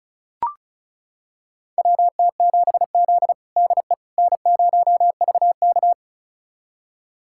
Morse code, heard as a keyed tone, about 36 wpm, 700 Hz. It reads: WT7Z DE N0VK
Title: USA Callsign DE Callsign - 4 Characters - 35wpm